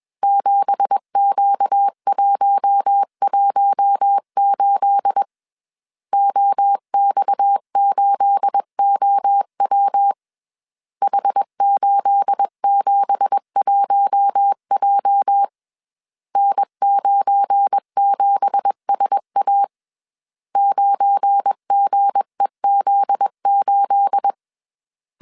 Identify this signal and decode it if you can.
Morse code, heard as a keyed tone, about 21 wpm, 800 Hz. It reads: 7Q118 OX8OW H871J N97SA 9GEZ8
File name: CW / DET / 1 9 1500C S C 20wpm 2310